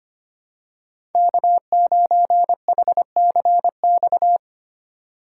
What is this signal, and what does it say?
Morse code, heard as a keyed tone, about 25 wpm, 700 Hz. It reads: K9HCX
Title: USA Call Signs - 5 Characters - 8x Character Spacing - 25wpm